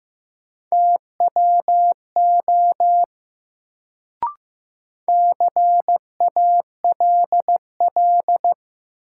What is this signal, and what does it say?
Morse code, heard as a keyed tone, about 15 wpm, 700 Hz. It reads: TWO CALL
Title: Sentences from Top 100 Words - New Word Drill - 15wpm